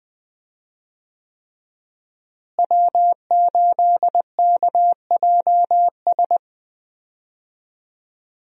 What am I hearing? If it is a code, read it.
Morse code, heard as a keyed tone, about 20 wpm, 700 Hz. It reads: W8KJS
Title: USA Call Signs - 5 Characters - Repeated 3x - 20wpm